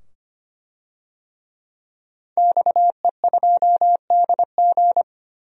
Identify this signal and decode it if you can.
Morse code, heard as a keyed tone, about 25 wpm, 690 Hz. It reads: XE2DG
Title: DXCC-100 Call Signs - 5 Characters - 25wpm